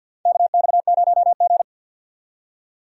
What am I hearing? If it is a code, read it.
Morse code, heard as a keyed tone, about 47 wpm, 700 Hz. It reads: KX0G
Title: USA Call Signs - 4 Characters - 50wpm